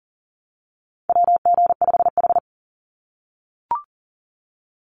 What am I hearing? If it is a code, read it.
Morse code, heard as a keyed tone, about 41 wpm, 700 Hz. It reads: WG5H
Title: Call Signs - 40wpm